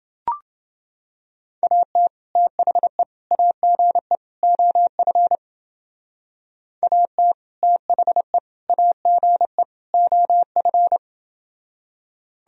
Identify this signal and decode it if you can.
Morse code, heard as a keyed tone, about 28 wpm, 700 Hz. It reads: AT THE AGE OF AT THE AGE OF